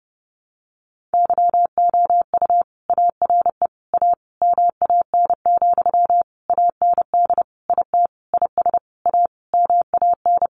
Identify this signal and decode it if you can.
Morse code, heard as a keyed tone, about 30 wpm, 700 Hz. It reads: YOU ARE A MAN, AND IT IS A MAN
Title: Sentences from Top 200 Words 30wpm